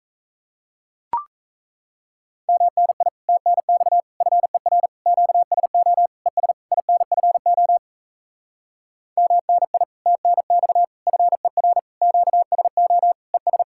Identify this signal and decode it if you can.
Morse code, heard as a keyed tone, about 37 wpm, 700 Hz. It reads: MNI TNX FER QSO ES INFO MNI TNX FER QSO ES